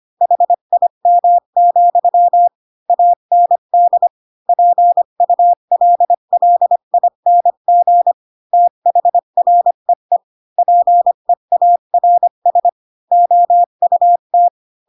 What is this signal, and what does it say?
Morse code, heard as a keyed tone, about 25 wpm, 700 Hz. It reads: HIM, AND PULLING THREE PEARS OUT
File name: Pinocchio / Pinocchio ch7 20 25 ik1hge